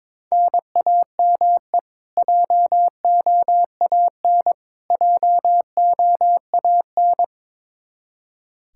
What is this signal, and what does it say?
Morse code, heard as a keyed tone, about 22 wpm, 700 Hz. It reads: NAME JOAN JOAN
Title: Practice QSO - Exchange 1 - Beginning - 22wpm